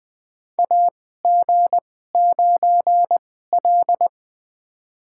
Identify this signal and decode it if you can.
Morse code, heard as a keyed tone, about 20 wpm, spaced as for 15 wpm, 700 Hz. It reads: AG9L
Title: USA Call Signs - 4 Characters - 2x Character Spacing - 20wpm